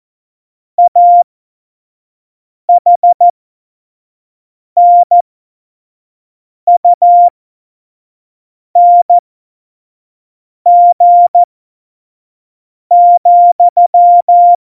Morse code, heard as a keyed tone, about 14 wpm, spaced as for 5 wpm, 700 Hz. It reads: AHNUNG,
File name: Morsekurs / Witze / Witz 03. Oktober 17, Tempo 25 F70